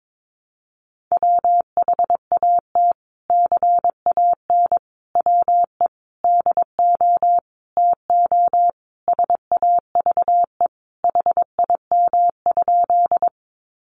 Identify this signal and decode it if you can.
Morse code, heard as a keyed tone, about 22 wpm, 700 Hz. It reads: WHAT CAN WE DO TO SAVE HIM?
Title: Sentences from Top 1100 Words 22wpm